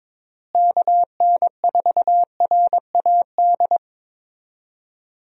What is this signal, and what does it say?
Morse code, heard as a keyed tone, about 22 wpm, 700 Hz. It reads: KN4RAD
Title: USA Call Signs - 6 Characters - Repeated 2x - 22wpm